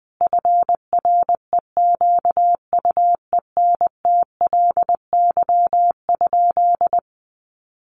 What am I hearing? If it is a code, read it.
Morse code, heard as a keyed tone, about 20 wpm, 700 Hz. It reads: FREQUENTLY?
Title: Sentences from Top 1300 Words 20wpm